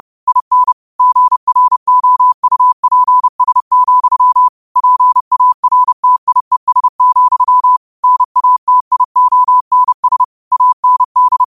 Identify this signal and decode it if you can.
Morse code, heard as a keyed tone, about 30 wpm, 1000 Hz. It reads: IN GROUPS, PARTIES, NATIONS AND
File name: qotdfiles / 30wpm